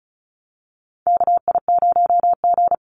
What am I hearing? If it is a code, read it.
Morse code, heard as a keyed tone, about 36 wpm, 700 Hz. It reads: KI0G
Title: NA SOTA Chaser Callsigns 35wpm